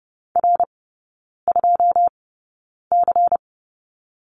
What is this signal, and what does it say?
Morse code, heard as a keyed tone, about 30 wpm, spaced as for 9 wpm, 700 Hz. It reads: R2C
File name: Lesson 17e - Letter u - Three Characters 30wpm